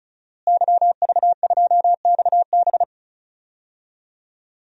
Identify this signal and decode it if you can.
Morse code, heard as a keyed tone, about 36 wpm, 700 Hz. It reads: YV2XB
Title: DXCC-100 Call Signs - 5 Characters - Repeated 2x - 35wpm